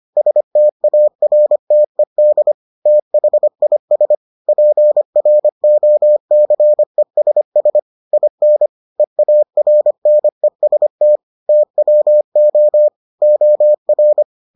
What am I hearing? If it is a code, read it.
Morse code, heard as a keyed tone, about 25 wpm, 600 Hz. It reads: STARTED THIS PROCESS IN EARNEST TWO OR